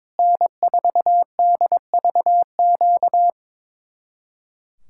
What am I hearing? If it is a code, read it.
Morse code, heard as a keyed tone, about 22 wpm, 700 Hz. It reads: N4DVQ
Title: Speed-Racing Common QSO Phrases - 22wpm